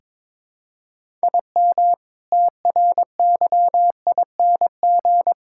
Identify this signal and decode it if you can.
Morse code, heard as a keyed tone, about 22 wpm, 700 Hz. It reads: IM TRYING
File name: practice / 22wpm / Sentences with 4-Word Phrases - 22wpm